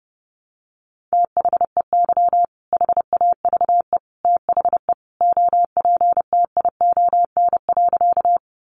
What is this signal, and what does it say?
Morse code, heard as a keyed tone, about 30 wpm, 700 Hz. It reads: THEY HAVE THE OPTION.
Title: Sentences from Top 1300 Words 30wpm